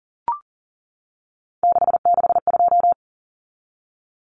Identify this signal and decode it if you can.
Morse code, heard as a keyed tone, about 40 wpm, 700 Hz. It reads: 662
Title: Common Cut Numbers - 3-Digit 40wpm